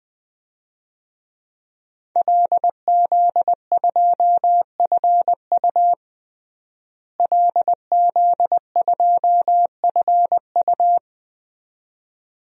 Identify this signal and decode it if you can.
Morse code, heard as a keyed tone, about 20 wpm, 700 Hz. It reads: LZ2FU LZ2FU
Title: DXCC-100 Call Signs - 5 Characters - Repeated 2x - 20wpm